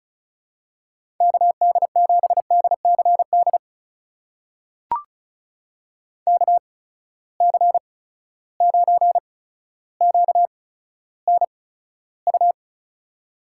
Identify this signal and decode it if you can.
Morse code, heard as a keyed tone, about 36 wpm, spaced as for 15 wpm, 700 Hz. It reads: KD7DCD KC9QNU
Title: USA Call Signs - 6 Characters - 8x Character Spacing - 35wpm